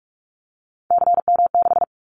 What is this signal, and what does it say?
Morse code, heard as a keyed tone, about 46 wpm, 700 Hz. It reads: CM6
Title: Callsign Prefixes for Top 100 DXCC - 45wpm